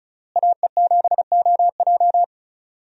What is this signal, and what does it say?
Morse code, heard as a keyed tone, about 34 wpm, 700 Hz. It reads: AE7OJ
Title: USA Call Signs - 5 Characters - Repeated 3x - 35wpm